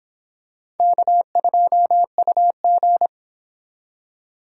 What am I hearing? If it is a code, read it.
Morse code, heard as a keyed tone, about 26 wpm, 700 Hz. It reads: K2UG